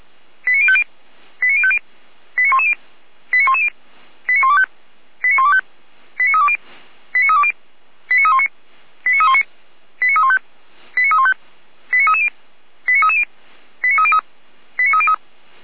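Five-tone selective call (ZVEI), heard as quick runs of five tones, tones of about 70 ms. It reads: 89050 89050 89110 89110 89125 89125 89320 89320 89319 89319 89315 89315 89330 89330 89303 89303
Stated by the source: ZVEI five-tone call received by SDR.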